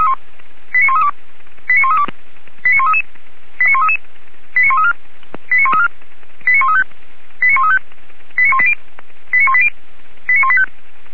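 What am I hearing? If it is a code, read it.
Five-tone selective call (ZVEI), heard as quick runs of five tones, tones of about 70 ms. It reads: 89132 89132 89133 89133 89134 89134 89135 89135 89180 89180 89185
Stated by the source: Five-tone ZVEI selective call received on SDR.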